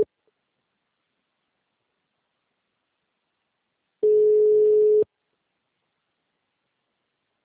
Ringback tone (Europe, 425 Hz).